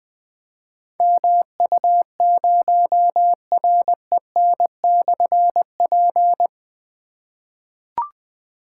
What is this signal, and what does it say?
Morse code, heard as a keyed tone, about 20 wpm, 700 Hz. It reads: MU0REN/P
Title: Difficult Call Signs 20wpm